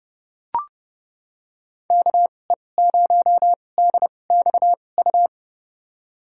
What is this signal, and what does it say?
Morse code, heard as a keyed tone, about 30 wpm, spaced as for 22 wpm, 700 Hz. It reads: KE0DXU
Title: USA Call Signs - 6 Characters - 2x Character Spacing - 30wpm